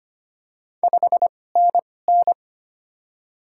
Morse code, heard as a keyed tone, about 25 wpm, spaced as for 18 wpm, 700 Hz. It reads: 5NN
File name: Top 20 QSO Elements - 2x Character Spacing - 25wpm